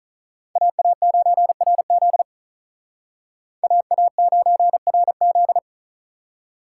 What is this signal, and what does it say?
Morse code, heard as a keyed tone, about 45 wpm, 700 Hz. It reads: AA9RZ AA9RZ